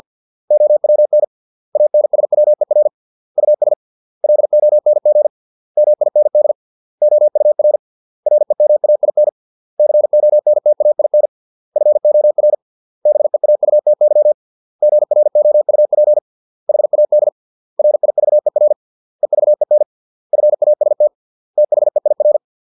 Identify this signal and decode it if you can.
Morse code, heard as a keyed tone, about 50 wpm, 600 Hz. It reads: OWN ANSWER US LONG MIND OUR REMAIN CONTAIN FOR BEAUTY GROUP HAD RIVER EVEN FAST THEIR